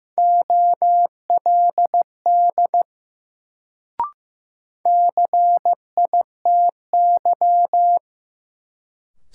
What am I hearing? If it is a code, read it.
Morse code, heard as a keyed tone, about 15 wpm, 700 Hz. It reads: OLD CITY
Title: Sets of 2 Words - New Word Drill - 15wpm